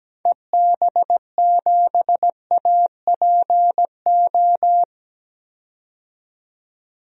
Morse code, heard as a keyed tone, about 17 wpm, 700 Hz. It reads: EB7APO
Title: DXCC-100 Call Signs - 6 Characters - 17wpm